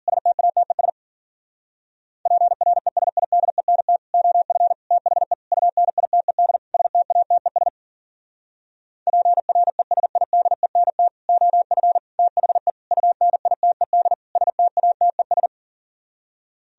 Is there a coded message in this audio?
Yes: Morse code from 0.1 to 15.5 s, heard as a keyed tone, about 50 wpm, 700 Hz. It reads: STATES PRESIDENT OF THE UNITED STATES PRESIDENT OF THE UNITED STATES